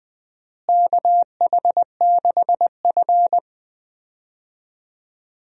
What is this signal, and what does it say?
Morse code, heard as a keyed tone, about 21 wpm, 700 Hz. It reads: KH6F
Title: USA Call Signs - 4 Characters - Repeated 2x - 20wpm